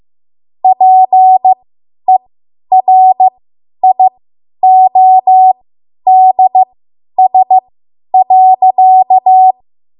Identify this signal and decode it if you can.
Morse code, heard as a keyed tone, about 15 wpm, spaced as for 10 wpm, 750 Hz. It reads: PERIODS.
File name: Morse / 10WPM